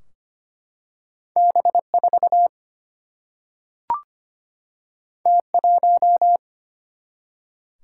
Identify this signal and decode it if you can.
Morse code, heard as a keyed tone, about 25 wpm, 700 Hz. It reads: B4 T1
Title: Single-Letter Single-Number 25wpm